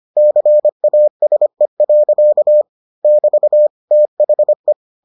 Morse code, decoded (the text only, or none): CASE. <BT> THE